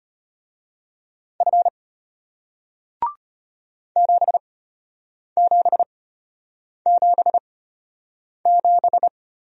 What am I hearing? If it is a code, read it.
Morse code, heard as a keyed tone, about 34 wpm, spaced as for 8 wpm, 700 Hz. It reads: F 7777